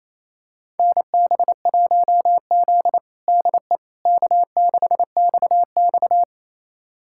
Morse code, heard as a keyed tone, about 28 wpm, 700 Hz. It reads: NB1Z DE K6XX
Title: USA Callsign DE Callsign - 4 Characters - 28wpm